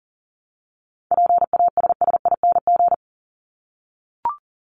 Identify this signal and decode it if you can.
Morse code, heard as a keyed tone, about 40 wpm, 700 Hz. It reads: PASSING
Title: ING Suffix - Top 100 Words with Base Words 40wpm